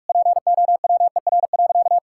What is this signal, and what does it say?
Morse code, heard as a keyed tone, about 42 wpm, 700 Hz. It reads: POWER.
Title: Sentences from Top 1000 Words - Review Part 5 - 45wpm